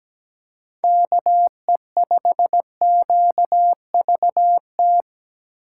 Morse code, heard as a keyed tone, about 17 wpm, 700 Hz. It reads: KE5QVT
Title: USA Call Signs - 4x Character Spacing - 17wpm